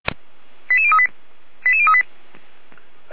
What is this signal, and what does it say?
Five-tone selective call (ZVEI), heard as quick runs of five tones, tones of about 80 ms. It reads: 88038 88038
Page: SDR/ZVEI